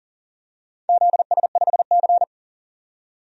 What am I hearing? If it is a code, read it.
Morse code, heard as a keyed tone, about 40 wpm, 700 Hz. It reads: ZS5C
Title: DXCC-100 Call Signs - 4 Characters - 40wpm